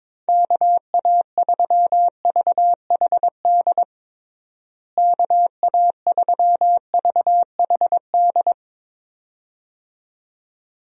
Morse code, heard as a keyed tone, about 22 wpm, 700 Hz. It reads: KA3VHD KA3VHD
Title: USA Call Signs - 6 Characters - Repeated 2x - 22wpm